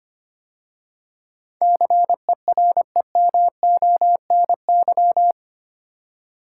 Morse code, heard as a keyed tone, about 25 wpm, 700 Hz. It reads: CEREMONY
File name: 8 Letter Words - First 3 Letters Given - 25wpm